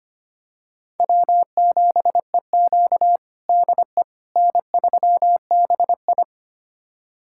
Morse code, heard as a keyed tone, about 25 wpm, 700 Hz. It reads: W7EQ DE N3BI
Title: USA Callsign DE Callsign - 4 Characters - 25wpm